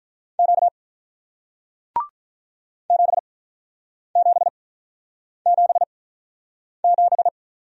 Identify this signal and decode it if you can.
Morse code, heard as a keyed tone, about 48 wpm, 700 Hz. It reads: Q 7 7 7 7